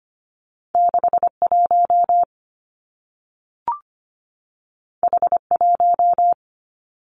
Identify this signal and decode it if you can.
Morse code, heard as a keyed tone, about 25 wpm, 700 Hz. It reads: 61 H1
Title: Two Challenging Letter-Number Combinations 25wpm